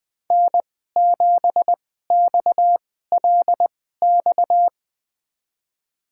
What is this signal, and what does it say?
Morse code, heard as a keyed tone, about 20 wpm, spaced as for 15 wpm, 700 Hz. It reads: N7XLX
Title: USA Call Signs - 2x Character Spacing - 20wpm